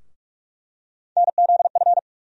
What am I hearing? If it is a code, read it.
Morse code, heard as a keyed tone, about 46 wpm, 700 Hz. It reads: NZF
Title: USA Call Sign Postfixes - 3 Letters 45wpm